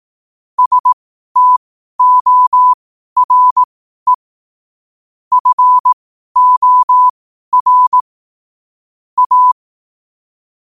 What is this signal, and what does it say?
Morse code, heard as a keyed tone, about 18 wpm, spaced as for 12 wpm, 1000 Hz. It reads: STORE FOR A
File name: qotdfiles / 12wpm